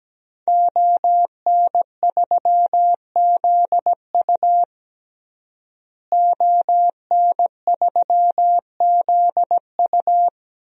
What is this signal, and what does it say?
Morse code, heard as a keyed tone, about 17 wpm, 700 Hz. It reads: ON3ZU ON3ZU
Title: DXCC-100 Call Signs - 5 Characters - Repeated 3x - 17wpm